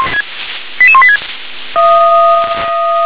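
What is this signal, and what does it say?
Five-tone selective call (ZVEI), heard as quick runs of five tones, tones of about 70 ms. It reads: 88186
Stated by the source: ZVEI five-tone call 88253 received via SDR.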